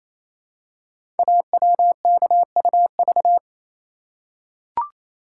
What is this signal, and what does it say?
Morse code, heard as a keyed tone, about 29 wpm, 700 Hz. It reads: AWKUV
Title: Code Groups - 5 Letters - 28wpm